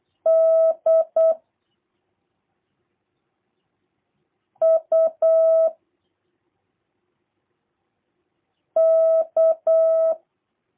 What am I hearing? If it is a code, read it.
Morse code, heard as a keyed tone, about 8 wpm, 650 Hz. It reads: D U K
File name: jh8cbh / 2023cw / review1-3